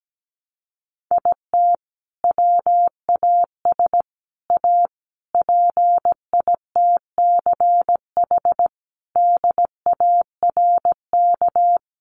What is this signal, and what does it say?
Morse code, heard as a keyed tone, about 17 wpm, 700 Hz. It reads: IT WAS A PITCH DARK
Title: Sentences from Top 1300 Words 17wpm